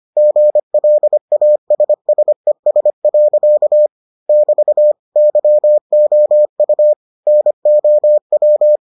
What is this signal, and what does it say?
Morse code, heard as a keyed tone, about 25 wpm, 600 Hz. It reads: GLASSES. <BT> YOU NOW